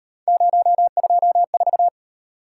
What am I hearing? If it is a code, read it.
Morse code, heard as a keyed tone, about 38 wpm, 700 Hz. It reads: 024